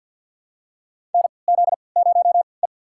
Morse code, heard as a keyed tone, about 52 wpm, spaced as for 29 wpm, 700 Hz. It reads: NZ0E
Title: USA Call Signs - 4 Characters - 3x Character Spacing - 50wpm